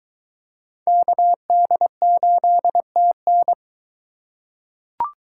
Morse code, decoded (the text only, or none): KD8TN